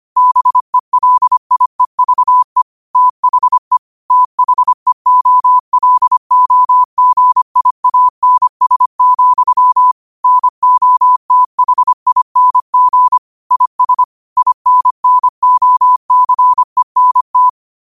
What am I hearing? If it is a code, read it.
Morse code, heard as a keyed tone, about 25 wpm, 1000 Hz. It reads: BELIEVE THE THEOLOGIANS, NOTHING IS INNOCENT